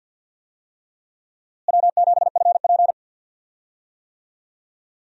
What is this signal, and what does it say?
Morse code, heard as a keyed tone, about 51 wpm, 700 Hz. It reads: W7FP
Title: USA Call Signs - 4 Characters - Repeated 3x - 50wpm